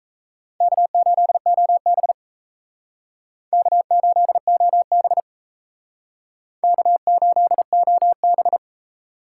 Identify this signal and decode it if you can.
Morse code, heard as a keyed tone, about 35 wpm, 700 Hz. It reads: K8OB K8OB K8OB